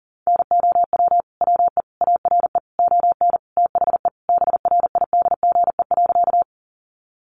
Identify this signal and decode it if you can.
Morse code, heard as a keyed tone, about 40 wpm, 700 Hz. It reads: NOW WE ARE ON THE BRIDGE.
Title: Sentences from Top 1200 Words 40wpm